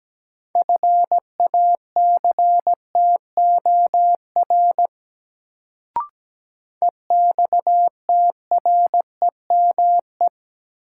Morse code, heard as a keyed tone, about 17 wpm, 700 Hz. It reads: FACTOR EXTREME
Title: Sentences from Top 1300 Words - New Word Drill - 17wpm